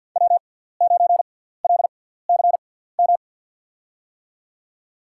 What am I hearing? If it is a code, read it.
Morse code, heard as a keyed tone, about 48 wpm, spaced as for 17 wpm, 700 Hz. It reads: W9LCM
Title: USA Call Signs - 5 Characters - 6x Character Spacing - 50wpm